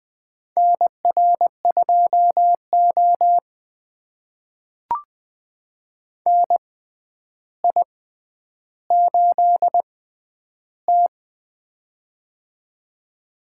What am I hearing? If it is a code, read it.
Morse code, heard as a keyed tone, about 20 wpm, spaced as for 10 wpm, 700 Hz. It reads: NR2O NI8T
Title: USA Call Signs - 6x Character Spacing - 20wpm